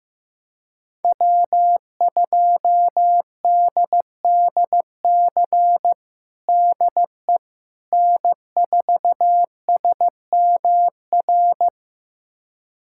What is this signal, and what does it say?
Morse code, heard as a keyed tone, about 15 wpm, 700 Hz. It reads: W2DDC DE N4SMR
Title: USA Callsign DE Callsign - 5 Characters - 15wpm